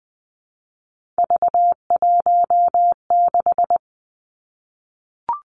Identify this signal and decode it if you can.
Morse code, heard as a keyed tone, about 20 wpm, 700 Hz. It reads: V16
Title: Three Challenging Letter-Number Combinations 20wpm